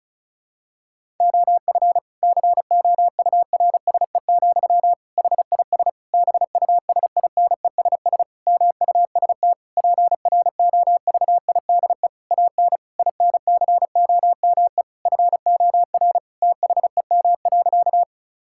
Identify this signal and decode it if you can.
Morse code, heard as a keyed tone, about 36 wpm, 700 Hz. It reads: OF COURSE, HIS BUSINESS MUST PROVIDE AN INCOME FOR THEM.
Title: Sentences from Top 1000 Words - Review Part 5 - 35wpm